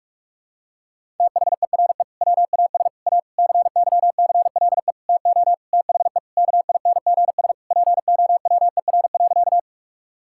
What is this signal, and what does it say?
Morse code, heard as a keyed tone, about 42 wpm, 700 Hz. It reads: THERE WAS A CYCLE TO THE KINGS POWER.
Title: Sentences from Top 1000 Words - Review Part 6 - 45wpm